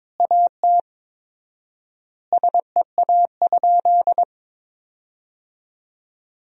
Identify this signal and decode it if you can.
Morse code, heard as a keyed tone, about 22 wpm, 700 Hz. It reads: AT SEA?
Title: 4x Word-Spacing - Sentences from Top 300 Words - 22wpm